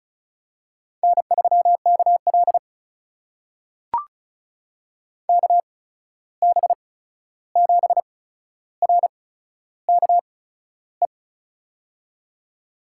Morse code, heard as a keyed tone, about 36 wpm, spaced as for 13 wpm, 700 Hz. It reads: N3KL KB7RKE
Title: USA Call Signs - 8x Character Spacing - 35wpm